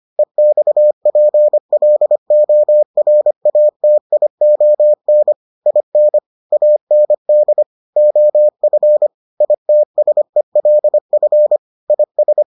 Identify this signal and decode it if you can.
Morse code, heard as a keyed tone, about 25 wpm, 600 Hz. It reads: EXPLORATION IN AND OF ITSELF IS